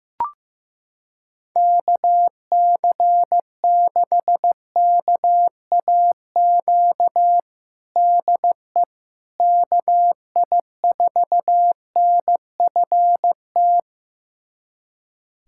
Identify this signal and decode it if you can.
Morse code, heard as a keyed tone, about 15 wpm, 700 Hz. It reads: KC6KAQ DE KI4NFT
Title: USA Callsign DE Callsign - 6 Characters - 15wpm